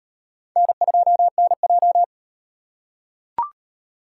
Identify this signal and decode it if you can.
Morse code, heard as a keyed tone, about 37 wpm, 700 Hz. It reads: N2NJ